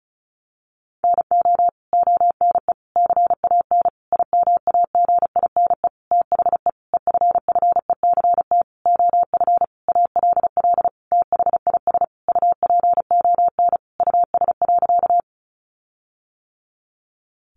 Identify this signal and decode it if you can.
Morse code, heard as a keyed tone, about 34 wpm, 700 Hz. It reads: NO ONE CAN IMAGINE THE EFFECT OF ALL THIS UPON US.
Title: Sentences from Top 1400 Words 35wpm